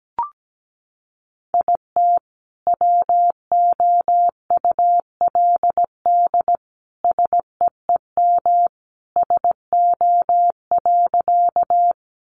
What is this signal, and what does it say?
Morse code, heard as a keyed tone, about 17 wpm, 700 Hz. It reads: IT WOULD SEEM SO.
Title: Sentences from Top 300 Words 17wpm